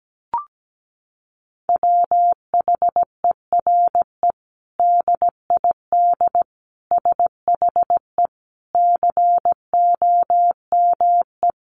Morse code, heard as a keyed tone, about 17 wpm, 700 Hz. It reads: WHERE DID SHE COME
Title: Sentences from Top 200 Words 17wpm